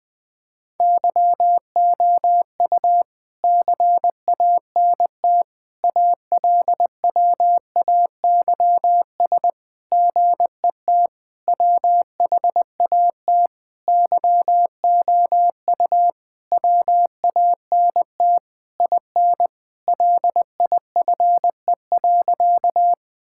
Morse code, heard as a keyed tone, about 20 wpm, 700 Hz. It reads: YOU CANT ALWAYS GET WHAT YOU WANT IN LIFE.
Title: Sentences with 7-Word Phrases - 20wpm